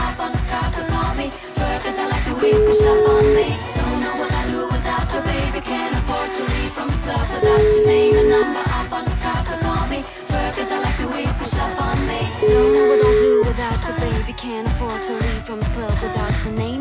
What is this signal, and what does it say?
Ringback tone (Europe, 425 Hz), with noise.